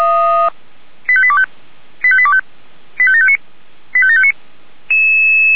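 Five-tone selective call (ZVEI), heard as quick runs of five tones, tones of about 70 ms. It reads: 85825 85825 85759 85759
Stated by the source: Five-tone ZVEI selective call received on an SDR.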